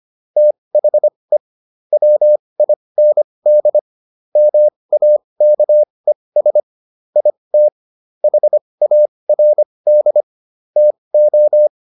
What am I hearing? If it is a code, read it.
Morse code, heard as a keyed tone, about 25 wpm, spaced as for 20 wpm, 600 Hz. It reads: THE WIND MAKES IT HARD TO